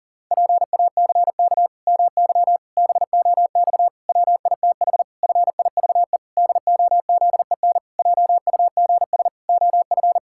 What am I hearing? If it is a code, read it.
Morse code, heard as a keyed tone, about 40 wpm, 700 Hz. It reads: PACK MY BOX WITH FIVE DOZEN JUGS OF
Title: Pangram Challenge v2 - 40wpm